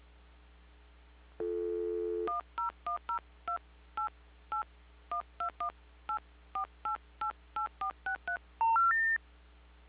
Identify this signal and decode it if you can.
DTMF, heard as telephone touch tones, tones of about 120 ms. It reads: 1010288 12184888463
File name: recordings / PortlandUnion Millennium